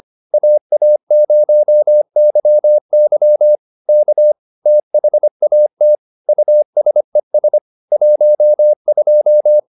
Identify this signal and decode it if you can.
Morse code, heard as a keyed tone, about 25 wpm, 600 Hz. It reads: AA0YY K THAT USES 12
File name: SS220 25